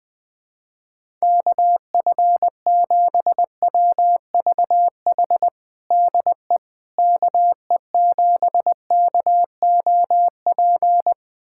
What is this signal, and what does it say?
Morse code, heard as a keyed tone, about 20 wpm, 700 Hz. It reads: KF7WVH DE KE7KOP
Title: USA Callsign DE Callsign - 6 Characters - 20wpm